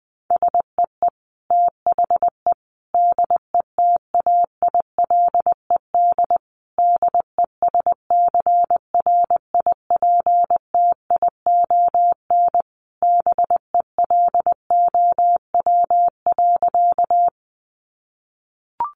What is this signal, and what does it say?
Morse code, heard as a keyed tone, about 20 wpm, 700 Hz. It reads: SEE THE DETAILED DESCRIPTION BELOW.
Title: Sentences from Top 1500 Words 20wpm